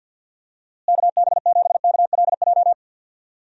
Morse code, heard as a keyed tone, about 47 wpm, 700 Hz. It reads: KB7KLJ
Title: USA Call Signs - 6 Characters - Repeated 2x - 50wpm